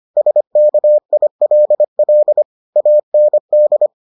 Morse code, heard as a keyed tone, about 25 wpm, 600 Hz. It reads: SKILL AND